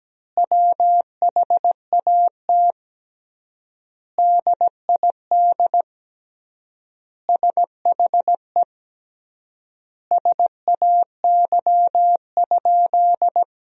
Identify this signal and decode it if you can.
Morse code, heard as a keyed tone, about 17 wpm, 700 Hz. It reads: WHAT DID SHE SAY?
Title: 3x Word-Spacing - Sentences from Top 500 Words - 17wpm